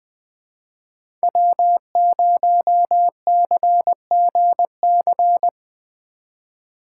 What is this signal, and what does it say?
Morse code, heard as a keyed tone, about 20 wpm, 700 Hz. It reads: W0CGC
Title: 2023 POTA Hunter Call Signs - 20wpm